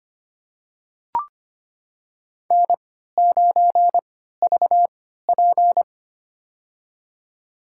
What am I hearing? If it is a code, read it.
Morse code, heard as a keyed tone, about 25 wpm, spaced as for 14 wpm, 700 Hz. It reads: N9VP
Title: USA Call Signs - 3x Character Spacing - 25wpm